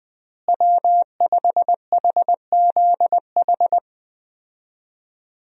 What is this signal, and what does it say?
Morse code, heard as a keyed tone, about 20 wpm, 700 Hz. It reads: W5HZH
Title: USA Call Signs - 5 Characters - Repeated 3x - 20wpm